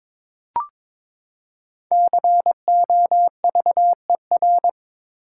Morse code, heard as a keyed tone, about 23 wpm, 700 Hz. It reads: COVER
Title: Five Letter Words - 22wpm